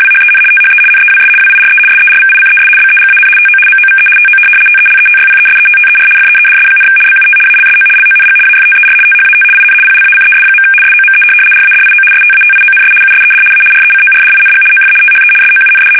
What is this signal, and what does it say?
RTTY, about 75 baud. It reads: WHEN THE NEWS WIRE SERVICES BEGAN TO MOVE TO TELETYPESETTER FEEDS, BUT ALSO LONG AFTER THAT IN PLACES. SOME RADIO STATIONS USED A RECORDING OF THE SOUND
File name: fsk850-75B / teletype-corp